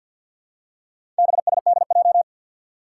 Morse code, heard as a keyed tone, about 51 wpm, 700 Hz. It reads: BSDJ
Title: Four Challenging Letter-Number Combinations 50wpm